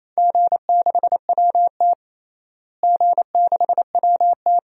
Morse code, heard as a keyed tone, about 28 wpm, 700 Hz. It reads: G6WT G6WT
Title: DXCC-100 Call Signs - 4 Characters - Repeated 2x - 28wpm